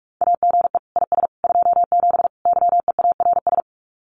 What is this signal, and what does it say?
Morse code, heard as a keyed tone, about 46 wpm, 700 Hz. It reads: AGE IS 27 YEARS
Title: Common QSO Phrases Encore 45wpm